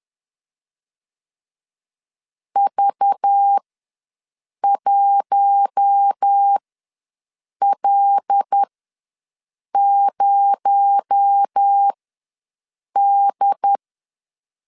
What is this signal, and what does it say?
Morse code, heard as a keyed tone, about 10 wpm, spaced as for 6 wpm, 800 Hz. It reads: V1L0D
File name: LEZIONI / 7 GF9Z / 7 G ws 10wpm 3843